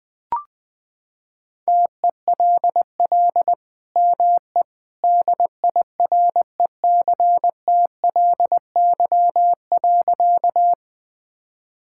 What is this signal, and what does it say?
Morse code, heard as a keyed tone, about 21 wpm, 700 Hz. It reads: TELL ME DIRECTLY.
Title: Sentences from Top 1000 Words - Review Part 5 - 20wpm